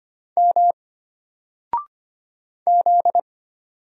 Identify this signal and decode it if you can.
Morse code, heard as a keyed tone, about 25 wpm, 700 Hz. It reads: M Z
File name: Single-Letter-Number-25wpm